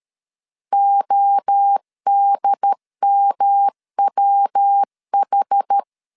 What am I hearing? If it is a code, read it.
Morse code, heard as a keyed tone, about 13 wpm, 800 Hz. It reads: ODMWH
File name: CW / DET / 1 3 1500C S C 12wpm 2750